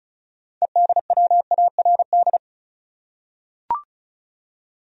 Morse code, heard as a keyed tone, about 36 wpm, 700 Hz. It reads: EDWARD
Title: USA Common Names - 35wpm